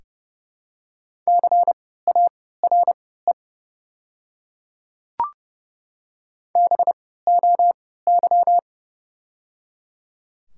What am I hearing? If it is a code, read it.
Morse code, heard as a keyed tone, about 31 wpm, spaced as for 17 wpm, 700 Hz. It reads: CARE BOY
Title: 3x Character Spacing - Top 500 Words - 30wpm